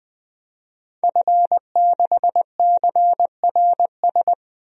Morse code, heard as a keyed tone, about 20 wpm, 700 Hz. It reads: F6CRS
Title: DXCC-100 Call Signs - 5 Characters - 20wpm